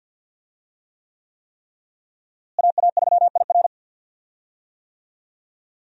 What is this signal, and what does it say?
Morse code, heard as a keyed tone, about 53 wpm, 700 Hz. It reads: AA3IR
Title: USA Call Signs - 5 Characters - Repeated 3x - 50wpm